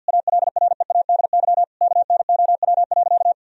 Morse code, heard as a keyed tone, about 52 wpm, 700 Hz. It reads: ALREADY KNOW.
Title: Sentences from Top 1000 Words - Review Part 1 - 50wpm